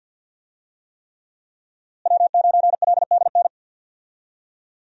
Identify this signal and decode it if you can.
Morse code, heard as a keyed tone, about 47 wpm, 690 Hz. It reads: W9LDN
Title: USA Call Signs - 5 Characters - Repeated 3x - 50wpm